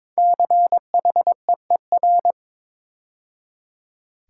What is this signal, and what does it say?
Morse code, heard as a keyed tone, about 22 wpm, 700 Hz. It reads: CHEER
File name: 5 Letter Words - First 3 Letters Given - 22wpm